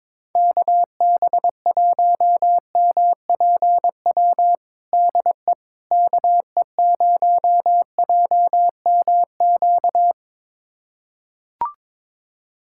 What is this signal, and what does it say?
Morse code, heard as a keyed tone, about 22 wpm, 700 Hz. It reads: KB1MPW DE KE0JMQ
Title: USA Callsign DE Callsign - 6 Characters - 22wpm